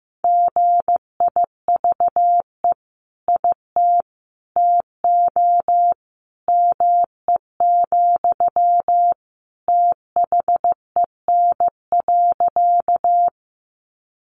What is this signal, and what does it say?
Morse code, heard as a keyed tone, about 15 wpm, 700 Hz. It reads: GIVE IT TO ME, THEN.
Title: Sentences from Top 200 Words 15wpm